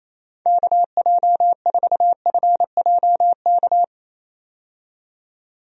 Morse code, heard as a keyed tone, about 27 wpm, 700 Hz. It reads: KJ4FJK
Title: USA Call Signs - 6 Characters - 28wpm